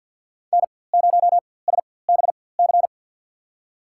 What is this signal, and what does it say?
Morse code, heard as a keyed tone, about 47 wpm, spaced as for 23 wpm, 700 Hz. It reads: N0SBC
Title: USA Call Signs - 5 Characters - 4x Character Spacing - 50wpm